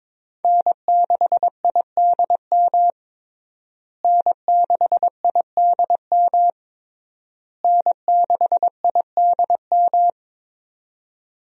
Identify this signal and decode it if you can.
Morse code, heard as a keyed tone, about 22 wpm, 690 Hz. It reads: N6IDM N6IDM N6IDM
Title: USA Call Signs - 5 Characters - Repeated 3x - 22wpm